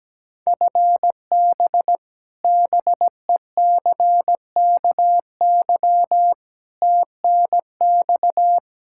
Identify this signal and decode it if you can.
Morse code, heard as a keyed tone, about 17 wpm, 700 Hz. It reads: FB BECKY TNX
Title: Practice QSO - Exchange 3 - More Info - 17wpm